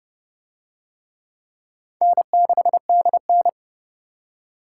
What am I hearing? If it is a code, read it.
Morse code, heard as a keyed tone, about 29 wpm, 700 Hz. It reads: N6DN